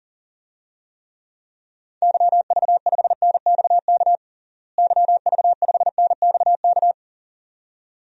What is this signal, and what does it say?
Morse code, heard as a keyed tone, about 40 wpm, 700 Hz. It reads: YV5NXK YV5NXK
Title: DXCC-100 Call Signs - 6 Characters - Repeated 2x - 40wpm